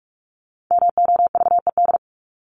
Morse code, heard as a keyed tone, about 44 wpm, 700 Hz. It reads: MOVED
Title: ED Suffix - Top 100 Words with Base Words 45wpm